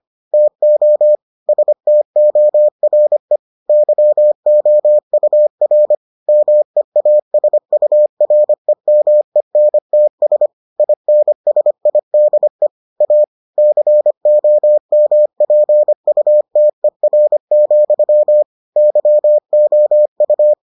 Morse code, heard as a keyed tone, about 25 wpm, 600 Hz. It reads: TO STORE YOUR MEASUREMENTS INSIDE A COMPUTER, YOU